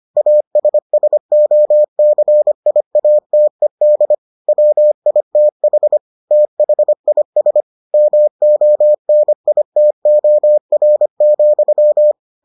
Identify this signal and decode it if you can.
Morse code, heard as a keyed tone, about 25 wpm, 600 Hz. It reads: ASSOCIATED WITH THIS MONITOR,